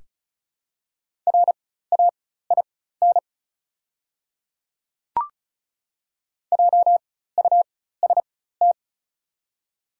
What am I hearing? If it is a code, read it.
Morse code, heard as a keyed tone, about 36 wpm, spaced as for 16 wpm, 690 Hz. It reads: RAIN JUST
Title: 4x Character Spacing - Top 300 Words - 35wpm